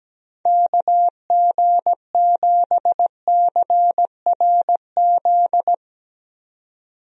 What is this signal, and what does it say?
Morse code, heard as a keyed tone, about 17 wpm, 700 Hz. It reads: KG7CRZ
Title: USA Call Signs - 6 Characters - 17wpm